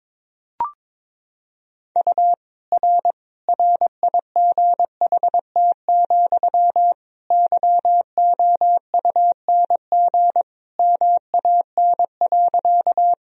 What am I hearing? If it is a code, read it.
Morse code, heard as a keyed tone, about 22 wpm, 700 Hz. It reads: U R RIGHT, YOUNG MAN.
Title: Sentences from Top 400 Words and Abbreviations 22wpm